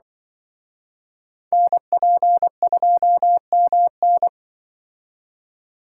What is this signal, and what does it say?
Morse code, heard as a keyed tone, about 25 wpm, 700 Hz. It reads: NP2MN
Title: Speed Racing - USA Call Signs 20wpm